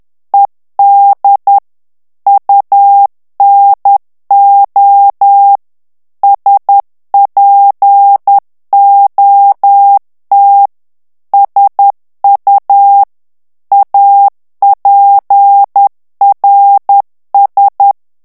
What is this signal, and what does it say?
Morse code, heard as a keyed tone, about 10 wpm, 800 Hz. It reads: ED UNO SPOT SU APRS
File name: morse mp3 / ITALIEN texte clair 1 vitesse 10 mots minute